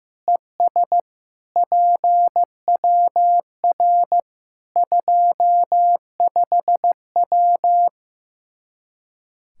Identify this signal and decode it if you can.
Morse code, heard as a keyed tone, about 16 wpm, 700 Hz. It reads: ES PWR 25W
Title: K1USN SST QSO Practice 15wpm